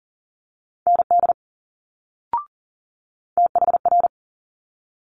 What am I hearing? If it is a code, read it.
Morse code, heard as a keyed tone, about 38 wpm, 700 Hz. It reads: ND THR
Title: Multi-letter Phonemes Quick Fire 40wpm